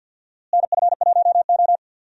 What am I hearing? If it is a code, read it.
Morse code, heard as a keyed tone, about 52 wpm, 700 Hz. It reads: NF1O
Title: USA Call Signs - 4 Characters - Repeated 2x - 50wpm